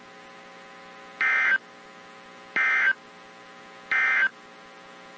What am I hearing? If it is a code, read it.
Emergency Alert System (SAME) tones, heard as harsh buzzing bursts. It is heard from 1.1 to 4.4 s.
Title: WZ2523 - NOAA Weather Radio